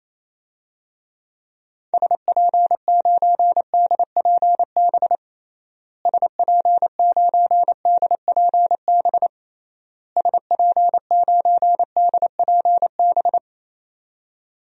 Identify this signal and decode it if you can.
Morse code, heard as a keyed tone, about 28 wpm, 700 Hz. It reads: SP9DPB SP9DPB SP9DPB
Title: DXCC-100 Call Signs - 6 Characters - Repeated 3x - 28wpm